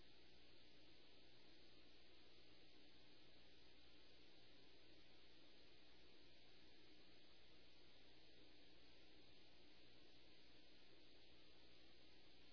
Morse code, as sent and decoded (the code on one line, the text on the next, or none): none